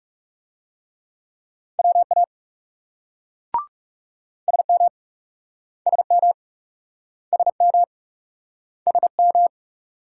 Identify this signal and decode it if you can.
Morse code, heard as a keyed tone, about 40 wpm, 700 Hz. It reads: WA SM SM SM SM